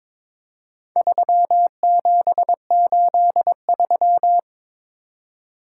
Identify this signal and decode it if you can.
Morse code, heard as a keyed tone, about 22 wpm, 700 Hz. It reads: 3783
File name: practice / 22wpm / Four Letter-Number Combinations - 22wpm